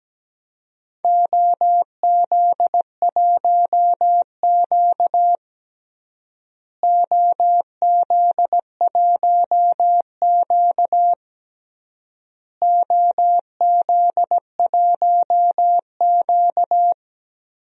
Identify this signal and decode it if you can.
Morse code, heard as a keyed tone, about 17 wpm, 700 Hz. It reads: OZ1Q OZ1Q OZ1Q
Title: DXCC-100 Call Signs - 4 Characters - Repeated 3x - 17wpm